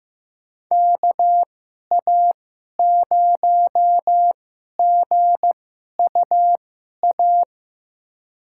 Morse code, heard as a keyed tone, about 15 wpm, spaced as for 11 wpm, 700 Hz. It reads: KA0GUA
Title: USA Call Signs - 6 Characters - 2x Character Spacing - 15wpm